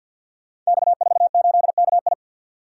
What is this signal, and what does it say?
Morse code, heard as a keyed tone, about 46 wpm, 700 Hz. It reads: X48KI
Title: Code Groups - 5 Letter:Numbers - 50wpm